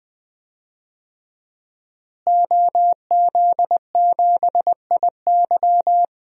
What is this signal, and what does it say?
Morse code, heard as a keyed tone, about 20 wpm, 700 Hz. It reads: OZ7IY
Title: DXCC-100 Call Signs - 5 Characters - Repeated 3x - 20wpm